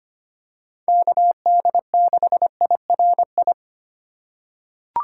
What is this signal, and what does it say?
Morse code, heard as a keyed tone, about 25 wpm, 700 Hz. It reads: KD6IRI
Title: K1USN SST QSO Practice - 25wpm